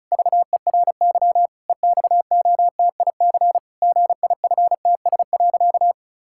Morse code, heard as a keyed tone, about 34 wpm, 700 Hz. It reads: VERY EXOTIC GIFTS.
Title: Pangram Challenge v2 - 35wpm